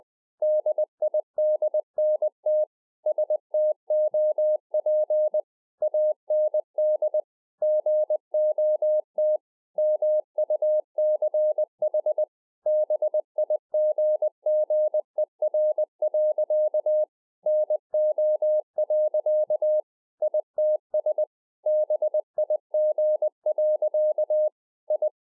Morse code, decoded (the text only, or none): DIDNT STOP AND GOT MUCH BIGGER. NO. ITS BIG. I